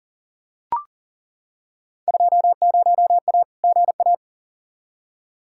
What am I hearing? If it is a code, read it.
Morse code, heard as a keyed tone, about 40 wpm, 700 Hz. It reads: 20A GA
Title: Field Day Exchanges Encore - 40wpm